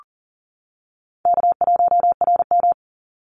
Morse code, heard as a keyed tone, about 41 wpm, 700 Hz. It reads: K1RM
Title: Easy Call Signs - 40wpm